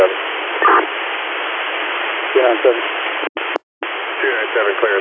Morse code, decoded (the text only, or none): none